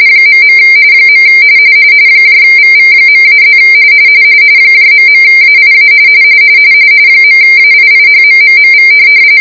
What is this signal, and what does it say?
RTTY, about 45.45 baud. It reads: .. VV.-+ ..AIA .AIIHMMMV